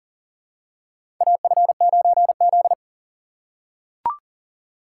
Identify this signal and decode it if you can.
Morse code, heard as a keyed tone, about 40 wpm, 700 Hz. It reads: AF9Z
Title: USA Call Signs - 4 Characters - 3x Character Spacing - 40wpm